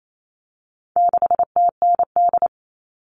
Morse code, heard as a keyed tone, about 28 wpm, 700 Hz. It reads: 6TND
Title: All Cut Numbers - 4-Digit 28wpm